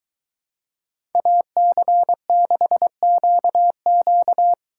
Morse code, heard as a keyed tone, about 22 wpm, 700 Hz. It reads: AC6QQ